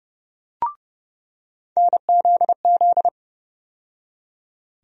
Morse code, heard as a keyed tone, about 29 wpm, 700 Hz. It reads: NZZ
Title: USA Call Sign Postfixes - 3 Letters 30wpm